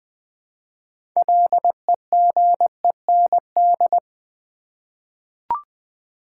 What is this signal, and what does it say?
Morse code, heard as a keyed tone, about 20 wpm, 700 Hz. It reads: LEGEND
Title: 6 Letter Words - First 3 Letters Given - 20wpm